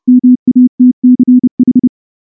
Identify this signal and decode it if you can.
Morse code, heard as a keyed tone, about 30 wpm, 250 Hz. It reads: MATCH
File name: morse / morse-training-30WPM